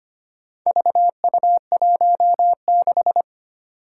Morse code, heard as a keyed tone, about 25 wpm, 700 Hz. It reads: VU16
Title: Four Challenging Letter-Number Combinations 25wpm